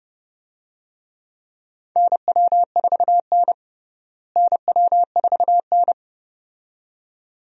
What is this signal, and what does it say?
Morse code, heard as a keyed tone, about 29 wpm, 700 Hz. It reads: NW4N NW4N
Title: USA Call Signs - 4 Characters - Repeated 2x - 30wpm